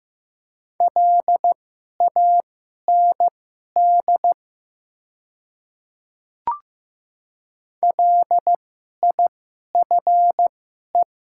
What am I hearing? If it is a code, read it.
Morse code, heard as a keyed tone, about 15 wpm, spaced as for 11 wpm, 690 Hz. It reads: LAND LIFE
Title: 2x Character Spacing - Top 300 Words - 15wpm